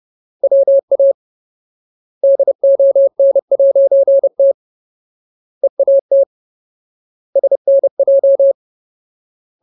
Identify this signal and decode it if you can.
Morse code, heard as a keyed tone, about 30 wpm, 550 Hz. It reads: WA DON'T EAT SNJ
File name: Ultimate Word List / 30WPM Ultimate